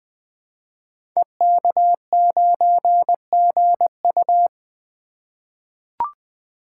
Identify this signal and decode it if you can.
Morse code, heard as a keyed tone, about 20 wpm, 700 Hz. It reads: EK9GU
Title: Code Groups - 5 Letter:Numbers - 20wpm